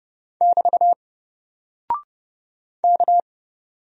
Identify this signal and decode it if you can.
Morse code, heard as a keyed tone, about 31 wpm, 700 Hz. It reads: <BT> K
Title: Single Character (Letter:Number:Punctuation:Prosign) 30wpm